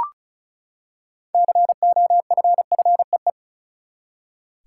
Morse code, heard as a keyed tone, about 36 wpm, 700 Hz. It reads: COFFEE
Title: Six Letter Words - 35wpm